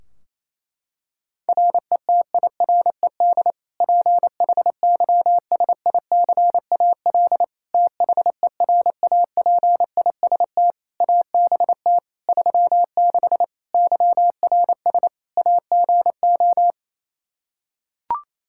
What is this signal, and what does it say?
Morse code, heard as a keyed tone, about 28 wpm, 700 Hz. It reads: RETIRED PHYSICAL THERAPIST ABT 36 YRS AGO
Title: Practice QSO - Exchange 3 - More Info - 28wpm